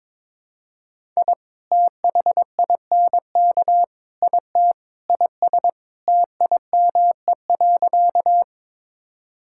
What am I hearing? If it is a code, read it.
Morse code, heard as a keyed tone, about 22 wpm, 700 Hz. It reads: I THINK IT IS TIME.
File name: practice / 22wpm / Sentences with 4-Word Phrases - 22wpm